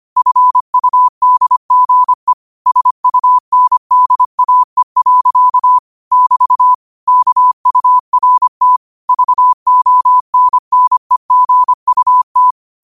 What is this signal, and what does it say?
Morse code, heard as a keyed tone, about 25 wpm, 1000 Hz. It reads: FUDGE SUNDAE. <BT> KURT VONNEGUT